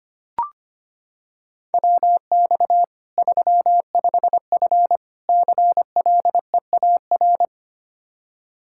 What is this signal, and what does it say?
Morse code, heard as a keyed tone, about 25 wpm, 700 Hz. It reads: WX 35F CLEAR
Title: K1USN SST QSO Practice 25wpm